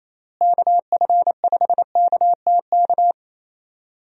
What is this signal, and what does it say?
Morse code, heard as a keyed tone, about 27 wpm, 700 Hz. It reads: KF5KTK
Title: USA Call Signs - 6 Characters - Repeated 3x - 28wpm